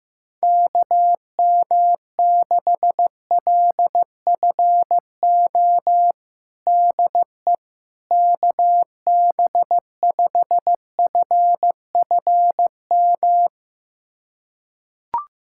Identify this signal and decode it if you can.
Morse code, heard as a keyed tone, about 16 wpm, 700 Hz. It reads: KM6LFO DE KB5FFM
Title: USA Callsign DE Callsign - 6 Characters - 15wpm